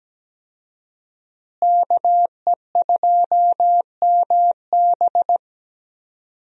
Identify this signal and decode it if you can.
Morse code, heard as a keyed tone, about 17 wpm, 700 Hz. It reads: KE2MB
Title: USA Call Signs - 5 Characters - Repeated 2x - 17wpm